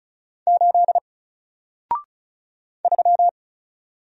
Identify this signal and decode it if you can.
Morse code, heard as a keyed tone, about 36 wpm, 700 Hz. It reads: 8 3
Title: Single Character (Letter:Number:Punctuation:Prosign) 35wpm